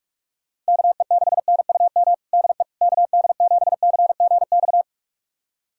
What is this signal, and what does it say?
Morse code, heard as a keyed tone, about 42 wpm, 700 Hz. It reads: KE6NUM DE KD7CGX
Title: USA Callsign DE Callsign - 6 Characters - 45wpm